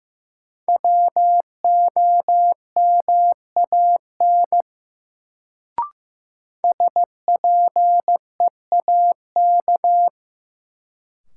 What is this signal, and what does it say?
Morse code, heard as a keyed tone, about 15 wpm, 700 Hz. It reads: WOMAN SPEAK
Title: Sentences from Top 1000 Words - Review Part 2 - New Word Drill - 15wpm